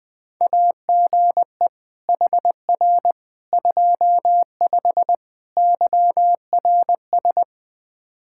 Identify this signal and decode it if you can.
Morse code, heard as a keyed tone, about 20 wpm, 700 Hz. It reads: AGE HR 25 YRS
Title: K1USN SST QSO Practice 20wpm